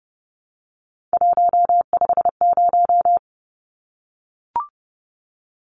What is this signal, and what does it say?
Morse code, heard as a keyed tone, about 29 wpm, 700 Hz. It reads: 150
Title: Numbers - 3-Digit 30wpm